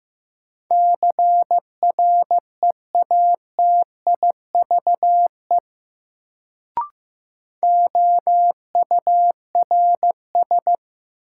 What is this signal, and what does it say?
Morse code, heard as a keyed tone, about 15 wpm, 700 Hz. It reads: CREATIVE OURS
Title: Sentences from Top 1000 Words - Review Part 5 - New Word Drill - 15wpm